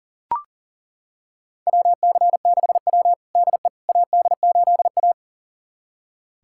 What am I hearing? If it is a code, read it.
Morse code, heard as a keyed tone, about 40 wpm, 700 Hz. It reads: WC6W DE AD8A
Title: USA Callsign DE Callsign - 4 Characters - 40wpm